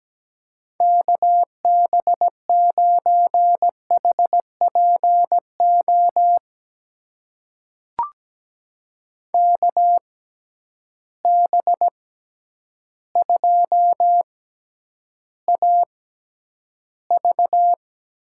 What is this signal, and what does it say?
Morse code, heard as a keyed tone, about 17 wpm, spaced as for 9 wpm, 700 Hz. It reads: KB9HPO KB2AV
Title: USA Call Signs - 6 Characters - 6x Character Spacing - 17wpm